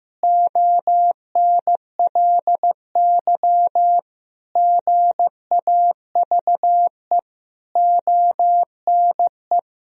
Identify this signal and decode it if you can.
Morse code, heard as a keyed tone, about 16 wpm, 700 Hz. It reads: ONLY GAVE ONE
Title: Sentences from Top 1000 Words - Review Part 2 - 15wpm